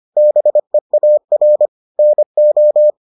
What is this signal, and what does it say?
Morse code, heard as a keyed tone, about 25 wpm, 600 Hz. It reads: BEAR NO